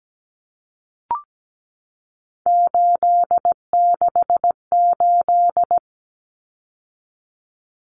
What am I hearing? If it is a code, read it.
Morse code, heard as a keyed tone, about 17 wpm, 700 Hz. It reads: 868
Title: Numbers - 3-Digit 17wpm